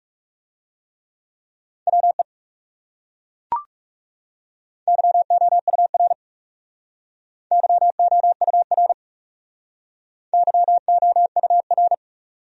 Morse code, heard as a keyed tone, about 36 wpm, 700 Hz. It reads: WE YOUR YOUR YOUR
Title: Speed-Racing Top 500 Words 30wpm